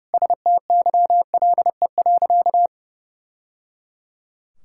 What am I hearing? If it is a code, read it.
Morse code, heard as a keyed tone, about 30 wpm, 700 Hz. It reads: STYLE.
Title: Sentences from Top 1000 Words - Review Part 5 - 30wpm